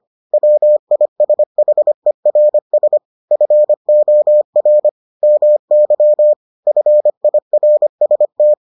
Morse code, heard as a keyed tone, about 25 wpm, 600 Hz. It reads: WISHERS FOR MY FIRST